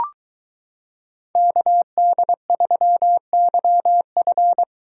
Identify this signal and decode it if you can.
Morse code, heard as a keyed tone, about 22 wpm, 700 Hz. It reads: KD3YF